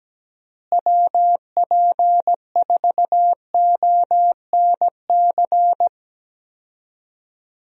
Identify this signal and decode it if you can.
Morse code, heard as a keyed tone, about 17 wpm, 700 Hz. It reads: WP4ONC
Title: USA Call Signs - 6 Characters - Repeated 3x - 17wpm